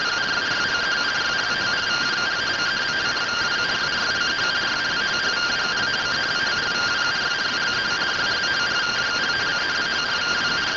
Two-tone teleprinter signal, about 170 Hz shift.